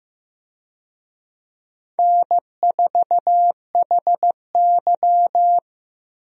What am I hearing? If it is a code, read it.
Morse code, heard as a keyed tone, about 16 wpm, 700 Hz. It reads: N4HY
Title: USA Call Signs - 4 Characters - Repeated 2x - 15wpm